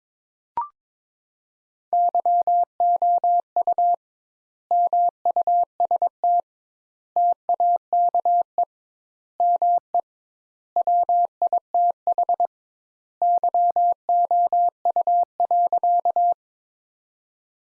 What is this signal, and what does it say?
Morse code, heard as a keyed tone, about 22 wpm, 700 Hz. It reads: YOU MUST TAKE ME WITH YOU.
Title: Sentences Top 200 Words - 22wpm